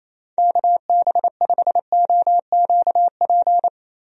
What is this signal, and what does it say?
Morse code, heard as a keyed tone, about 28 wpm, 700 Hz. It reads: KB5OQP
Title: USA Call Signs - 6 Characters - 28wpm